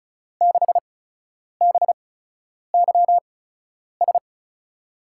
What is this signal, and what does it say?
Morse code, heard as a keyed tone, about 36 wpm, spaced as for 10 wpm, 700 Hz. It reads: 6BYS
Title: USA Call Signs - 6 Characters - 8x Character Spacing - 35wpm